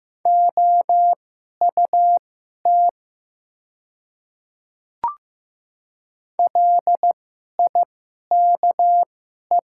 Morse code, heard as a keyed tone, about 15 wpm, spaced as for 11 wpm, 700 Hz. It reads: OUT LIKE
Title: 2x Character Spacing - Top 200 Words - 15wpm